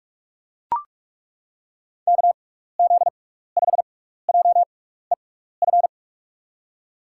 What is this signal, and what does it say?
Morse code, heard as a keyed tone, about 42 wpm, spaced as for 15 wpm, 700 Hz. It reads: KZ5JEF
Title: USA Call Signs - 6x Character Spacing - 45wpm